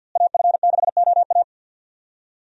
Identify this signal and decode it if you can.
Morse code, heard as a keyed tone, about 53 wpm, 700 Hz. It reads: AF6OA
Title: USA Call Signs - 5 Characters - Repeated 2x - 50wpm